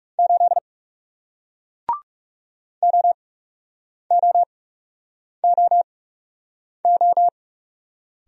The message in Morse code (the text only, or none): 8 OOOO